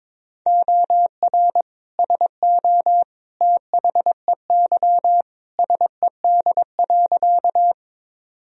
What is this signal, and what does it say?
Morse code, heard as a keyed tone, about 21 wpm, 700 Hz. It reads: OR SO THEY SED.
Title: Sentences from Top 100 Words and Abbreviations 22wpm